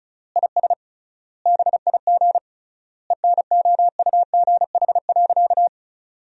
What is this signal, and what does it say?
Morse code, heard as a keyed tone, about 36 wpm, 700 Hz. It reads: IS BIG ENOUGH.
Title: 3x Word-Spacing - Sentences from Top 400 Words - 35wpm